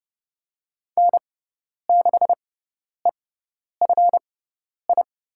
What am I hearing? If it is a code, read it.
Morse code, heard as a keyed tone, about 32 wpm, spaced as for 10 wpm, 690 Hz. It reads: N6EFI
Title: USA Call Signs - 5 Characters - 6x Character Spacing - 30wpm